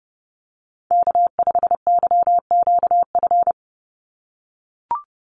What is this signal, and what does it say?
Morse code, heard as a keyed tone, about 30 wpm, 700 Hz. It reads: K5YQF
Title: Call Signs - 30wpm